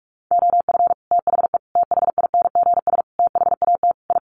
Morse code, heard as a keyed tone, about 44 wpm, 700 Hz. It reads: OF THE THINGS THAT I
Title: Sets of 5 Words Encore 45wpm